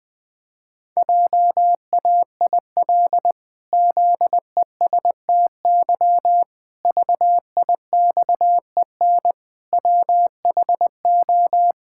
Morse code, heard as a keyed tone, about 20 wpm, 700 Hz. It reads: JAIL ZESTY VIXEN WHO
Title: Pangram Challenge v2 - 20wpm